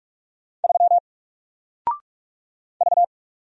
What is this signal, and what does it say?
Morse code, heard as a keyed tone, about 44 wpm, 700 Hz. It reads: 3 V
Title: Single Character (Letter:Number:Punctuation:Prosign) 45wpm